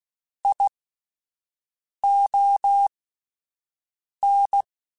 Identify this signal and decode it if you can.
Morse code, heard as a keyed tone, about 16 wpm, spaced as for 5 wpm, 800 Hz. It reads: ION
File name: Alphabetical / Alphabetical mp3s / Qs @5